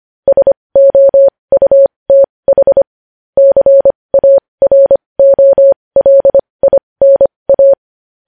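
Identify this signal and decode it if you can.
Morse code, heard as a keyed tone, about 25 wpm, spaced as for 20 wpm, 550 Hz. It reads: SOUTH CAROLINA
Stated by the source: Morse code news headlines practice at 20 wpm.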